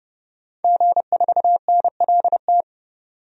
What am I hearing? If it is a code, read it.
Morse code, heard as a keyed tone, about 30 wpm, 700 Hz. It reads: G4NLT
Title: DXCC-100 Call Signs - 5 Characters - 30wpm